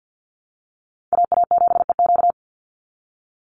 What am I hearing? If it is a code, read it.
Morse code, heard as a keyed tone, about 50 wpm, 700 Hz. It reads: AA7EQ
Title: NA SOTA Chaser Callsigns 50wpm